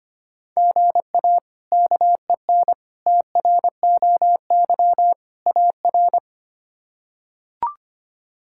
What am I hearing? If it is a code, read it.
Morse code, heard as a keyed tone, about 25 wpm, 700 Hz. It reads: GA KEN TROY AR
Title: K1USN SST QSO Practice - 25wpm